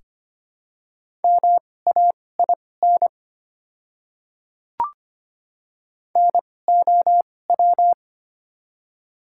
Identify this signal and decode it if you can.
Morse code, heard as a keyed tone, about 25 wpm, spaced as for 18 wpm, 690 Hz. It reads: MAIN NOW
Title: 2x Character Spacing - Top 300 Words - 25wpm